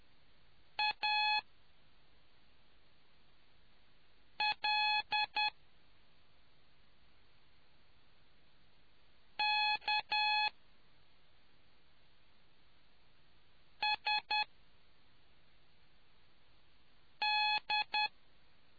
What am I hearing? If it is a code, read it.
Morse code, heard as a keyed tone, about 10 wpm, spaced as for 2 wpm, 850 Hz. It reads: ALKSD